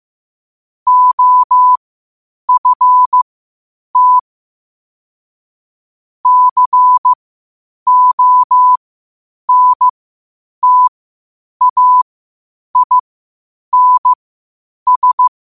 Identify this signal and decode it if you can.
Morse code, heard as a keyed tone, about 15 wpm, spaced as for 8 wpm, 1000 Hz. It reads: OFT CONTAINS
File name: qotdfiles / 8wpm